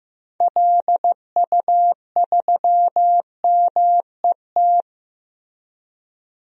Morse code, heard as a keyed tone, about 15 wpm, 700 Hz. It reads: LU3MET
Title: DXCC-100 Call Signs - 6 Characters - Repeated 3x - 15wpm